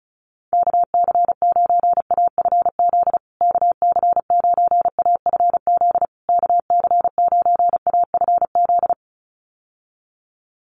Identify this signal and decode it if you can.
Morse code, heard as a keyed tone, about 34 wpm, 700 Hz. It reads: KC9AFZ KC9AFZ KC9AFZ
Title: USA Call Signs Repeated 3x 35wpm